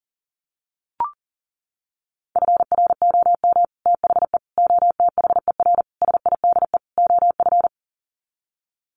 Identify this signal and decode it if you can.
Morse code, heard as a keyed tone, about 40 wpm, 700 Hz. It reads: FROM THE OTHER SIDE OF
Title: Sets of 5 Words Encore 40wpm